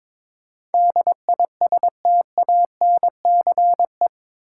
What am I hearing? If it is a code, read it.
Morse code, heard as a keyed tone, about 22 wpm, 700 Hz. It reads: DISTANCE
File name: practice / 22wpm / Eight Letter Words - 22wpm